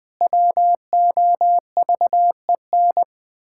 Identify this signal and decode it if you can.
Morse code, heard as a keyed tone, about 20 wpm, 700 Hz. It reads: WOVEN
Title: Pangram Challenge v2 - 20wpm